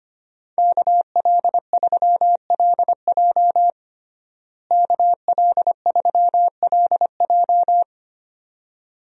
Morse code, heard as a keyed tone, about 25 wpm, 700 Hz. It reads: KL3LJ KL3LJ
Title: USA Call Signs - 5 Characters - Repeated 2x - 25wpm